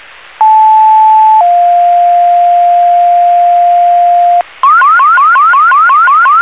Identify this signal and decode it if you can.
Two-tone page, heard as a one-second tone, then a three-second tone, about 870 then 700 Hz, with noise. It is heard from 0.4 to 4.4 s.